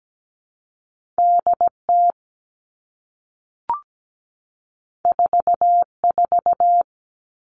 Morse code, heard as a keyed tone, about 17 wpm, 700 Hz. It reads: DT 44
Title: All Cut Numbers - 2-Digit 17wpm